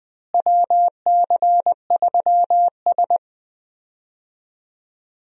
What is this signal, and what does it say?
Morse code, heard as a keyed tone, about 21 wpm, 700 Hz. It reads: WC3S
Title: USA Call Signs - 4 Characters - Repeated 3x - 20wpm